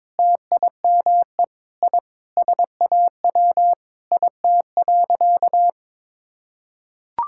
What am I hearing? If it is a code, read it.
Morse code, heard as a keyed tone, about 21 wpm, 700 Hz. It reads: TIME I SAW IT.
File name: practice / 22wpm / Sentences with 3-Word Phrases - 22wpm